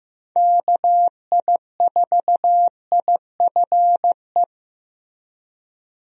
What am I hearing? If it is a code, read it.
Morse code, heard as a keyed tone, about 16 wpm, 700 Hz. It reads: KI4IFE
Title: USA Call Signs - 6 Characters - Repeated 3x - 15wpm